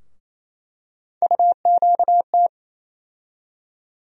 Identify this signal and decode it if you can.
Morse code, heard as a keyed tone, about 28 wpm, 700 Hz. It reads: UQT
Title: USA Call Sign Postfixes - 3 Letters 28wpm